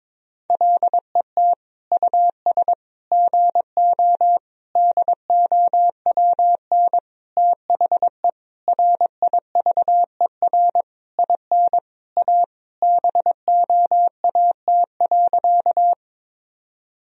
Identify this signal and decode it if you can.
Morse code, heard as a keyed tone, about 22 wpm, 700 Hz. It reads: LET US GO DOWN THE RIVER IN A BOAT.
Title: Sentences from Top 1000 Words - Review Part 3 - 22wpm